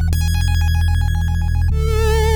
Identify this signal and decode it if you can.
There is no code.